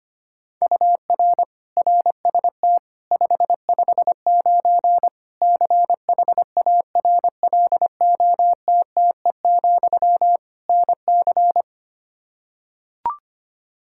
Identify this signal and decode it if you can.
Morse code, heard as a keyed tone, about 25 wpm, 700 Hz. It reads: UR RST 559 CHARLOTTE, NC
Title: Practice QSO - Exchange 1 - Beginning - 25wpm